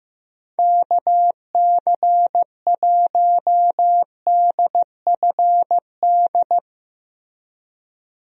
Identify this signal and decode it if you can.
Morse code, heard as a keyed tone, about 16 wpm, 700 Hz. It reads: KC1DFD
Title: USA Call Signs - 6 Characters - Repeated 2x - 15wpm